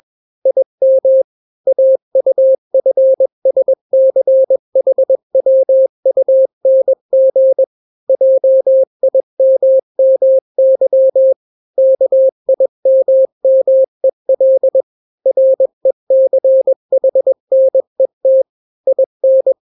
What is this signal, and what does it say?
Morse code, heard as a keyed tone, about 21 wpm, 530 Hz. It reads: IM AUFSCHWUNG JIMMY KIMMEL RECHNET IN
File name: daily / 20wpm